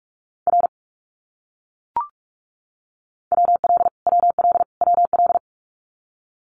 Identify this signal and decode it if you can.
Morse code, heard as a keyed tone, about 44 wpm, 700 Hz. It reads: R WL WL WL
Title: Abbreviations Repeated 3x - 45wpm